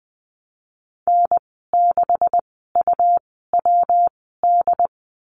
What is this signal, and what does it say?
Morse code, heard as a keyed tone, about 20 wpm, spaced as for 15 wpm, 700 Hz. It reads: N6UWD
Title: 147 Lesson 24g - Letter y - Call Signs 20wpm